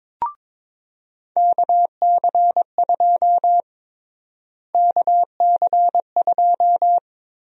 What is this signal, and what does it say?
Morse code, heard as a keyed tone, about 22 wpm, 700 Hz. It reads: KC2 KC2
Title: USA Call Sign Prefixes - Repeated 2x - 22wpm